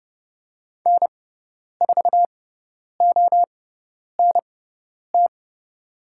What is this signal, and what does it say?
Morse code, heard as a keyed tone, about 31 wpm, spaced as for 10 wpm, 700 Hz. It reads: N4ONT
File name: Lesson 18g - Letter m - Call Signs 30wpm